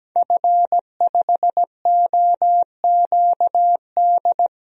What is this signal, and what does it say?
Morse code, heard as a keyed tone, about 17 wpm, 700 Hz. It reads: F5OQD
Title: DXCC-100 Call Signs - 5 Characters - 17wpm